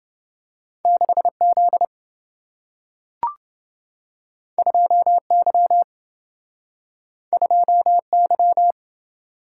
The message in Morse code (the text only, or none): 6Z 2Y 2Y